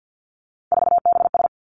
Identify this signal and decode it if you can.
Morse code, heard as a keyed tone, about 50 wpm, 700 Hz. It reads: 4BS
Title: Three Challenging Letter-Number Combinations 50wpm